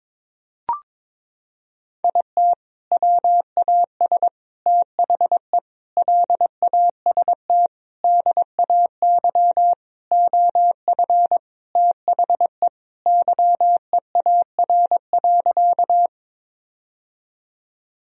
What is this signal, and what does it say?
Morse code, heard as a keyed tone, about 22 wpm, 700 Hz. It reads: IT WAS THE LAST DAY OF THE YEAR.
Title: Sentences from Top 1000 Words - Review Part 1 - 22wpm